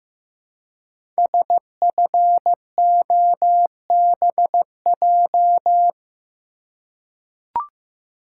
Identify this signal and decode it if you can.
Morse code, heard as a keyed tone, about 15 wpm, 700 Hz. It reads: SFOBJ
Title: Code Groups - 5 Letters - 15wpm